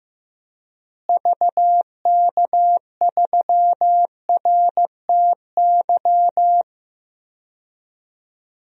Morse code, heard as a keyed tone, about 15 wpm, 700 Hz. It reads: VK3RTY
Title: DXCC-100 Call Signs - 6 Characters - Repeated 3x - 15wpm